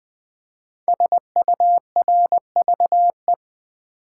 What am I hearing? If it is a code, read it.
Morse code, heard as a keyed tone, about 20 wpm, 700 Hz. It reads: SURVE
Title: Build A Word - 5 Letters At A Time - 12-Letter Words - 20wpm